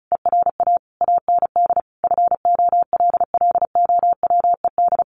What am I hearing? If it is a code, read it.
Morse code, heard as a keyed tone, about 34 wpm, 700 Hz. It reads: ERA AND FOLLOWED
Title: A Sound Of Thunder 35wpm